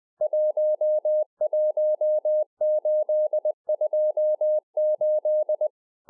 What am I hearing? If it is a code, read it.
Morse code, heard as a keyed tone, about 20 wpm, 610 Hz. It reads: 11828